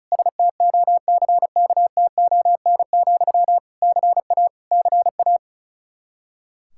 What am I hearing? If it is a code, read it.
Morse code, heard as a keyed tone, about 36 wpm, 700 Hz. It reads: STOCKTON, CA CA
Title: Practice QSO - Exchange 1 - Beginning - 35wpm